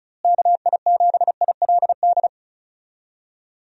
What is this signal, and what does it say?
Morse code, heard as a keyed tone, about 36 wpm, 700 Hz. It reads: KI7ILD
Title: USA Call Signs - 6 Characters - Repeated 3x - 35wpm